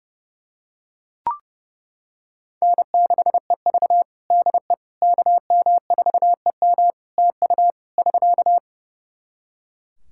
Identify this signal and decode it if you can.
Morse code, heard as a keyed tone, about 30 wpm, 700 Hz. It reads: N6EV DE KM4EM TU <SK>
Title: Practice QSO - Exchange 4 - Ending - 30wpm